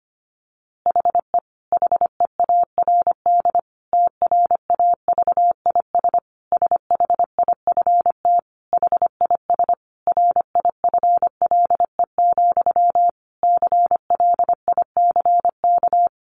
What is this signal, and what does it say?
Morse code, heard as a keyed tone, about 25 wpm, 700 Hz. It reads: HE HEARD TRAVIS SHIFT HIS RIFLE, CLICK
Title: A Sound Of Thunder 25wpm